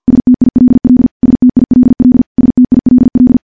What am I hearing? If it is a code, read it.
Morse code, heard as a keyed tone, about 50 wpm, 250 Hz. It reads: STILL STILL STILL
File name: morse-training-50WPM